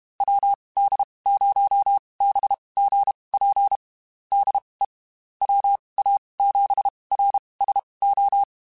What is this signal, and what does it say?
Morse code, heard as a keyed tone, about 32 wpm, spaced as for 23 wpm, 800 Hz. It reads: WD0BGP DE WA7RSO
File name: QSOs / QSOs-mp3s / WD0BGP 1@30